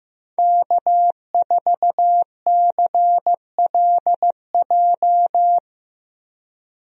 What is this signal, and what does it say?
Morse code, heard as a keyed tone, about 15 wpm, 700 Hz. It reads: K4CLJ
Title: USA Call Signs - 5 Characters - Repeated 3x - 15wpm